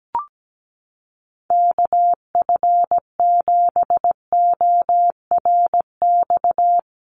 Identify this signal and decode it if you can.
Morse code, heard as a keyed tone, about 17 wpm, 700 Hz. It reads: KF7ORX
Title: USA Call Signs Repeated 3x 17wpm